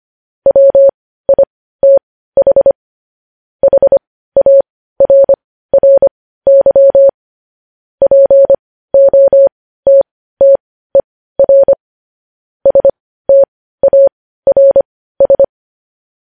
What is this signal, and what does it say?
Morse code, heard as a keyed tone, about 25 wpm, spaced as for 15 wpm, 550 Hz. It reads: WITH HARRY POTTER STARS